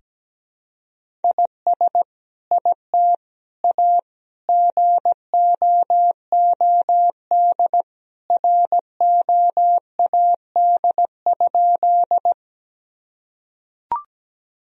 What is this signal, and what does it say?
Morse code, heard as a keyed tone, about 17 wpm, 700 Hz. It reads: IS IT A GOOD ROAD?
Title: Sentences from Top 1000 Words - Review Part 2 - 17wpm